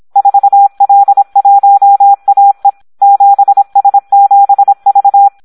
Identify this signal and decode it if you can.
Morse code, heard as a keyed tone, about 26 wpm, 800 Hz. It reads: 4L1AE 7S7V
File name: Audio practica CW / SD 1 26